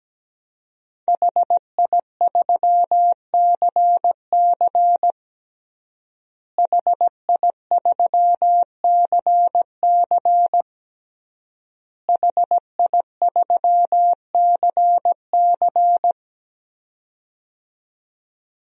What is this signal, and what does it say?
Morse code, heard as a keyed tone, about 17 wpm, 700 Hz. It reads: HI3CC HI3CC HI3CC
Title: DXCC-100 Call Signs - 5 Characters - Repeated 3x - 17wpm